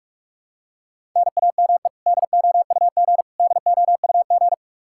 Morse code, heard as a keyed tone, about 46 wpm, 700 Hz. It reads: NAME DOUG DOUG
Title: Practice QSO - Exchange 1 - Beginning - 45wpm